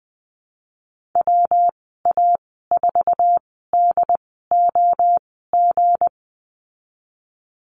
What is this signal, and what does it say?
Morse code, heard as a keyed tone, about 20 wpm, spaced as for 15 wpm, 700 Hz. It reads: WA4DOG
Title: 231 Lesson 37g - Letter x - Call Signs 20wpm